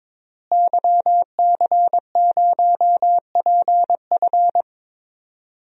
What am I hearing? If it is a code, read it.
Morse code, heard as a keyed tone, about 22 wpm, 700 Hz. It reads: YC0PF
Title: DXCC-100 Call Signs - 5 Characters - Repeated 2x - 22wpm